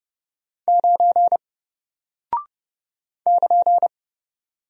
Morse code, heard as a keyed tone, about 30 wpm, 700 Hz. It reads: 9 <KN>
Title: Single Character (Letter:Number:Punctuation:Prosign) 30wpm